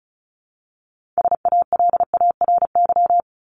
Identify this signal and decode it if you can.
Morse code, heard as a keyed tone, about 34 wpm, 700 Hz. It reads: SALARY
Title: Top 1751-2000 Words 35wpm